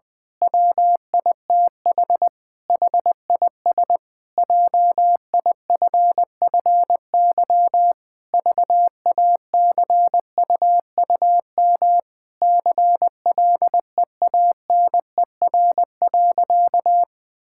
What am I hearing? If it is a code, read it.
Morse code, heard as a keyed tone, about 19 wpm, 700 Hz. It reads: WITH HIS JIFFY VACUUM CLEANER.
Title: Pangram Challenge v2 - 20wpm